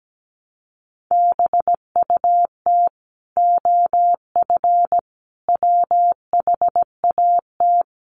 Morse code, heard as a keyed tone, about 17 wpm, 700 Hz. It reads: BUT OF WHAT
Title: Sentences from Top 700 Words 17wpm